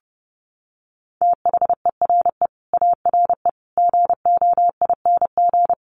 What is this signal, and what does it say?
Morse code, heard as a keyed tone, about 30 wpm, 700 Hz. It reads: THERE ARE GOING
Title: Sets of 5 Words 30wpm